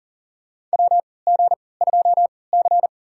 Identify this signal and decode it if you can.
Morse code, heard as a keyed tone, about 40 wpm, spaced as for 23 wpm, 700 Hz. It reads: WG2C
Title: USA Call Signs - 4 Characters - 3x Character Spacing - 40wpm